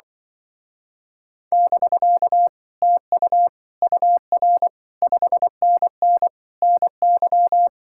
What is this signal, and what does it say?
Morse code, heard as a keyed tone, about 24 wpm, 700 Hz. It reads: <BK> TU UR 5NN NY
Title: Speed-Racing Common QSO Phrases - 20wpm